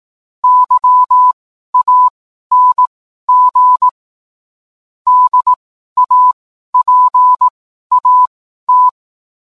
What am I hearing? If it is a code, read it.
Morse code, heard as a keyed tone, about 18 wpm, spaced as for 12 wpm, 1000 Hz. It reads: YANG DAPAT